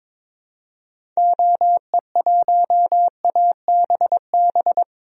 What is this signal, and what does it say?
Morse code, heard as a keyed tone, about 22 wpm, 700 Hz. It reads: OE1ABB
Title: DXCC-100 Call Signs - 6 Characters - 22wpm